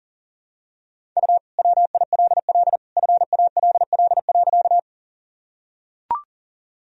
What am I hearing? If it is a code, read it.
Morse code, heard as a keyed tone, about 37 wpm, 700 Hz. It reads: U WILL FALL.
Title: Sentences from Top 500 Words and Abbreviations 40wpm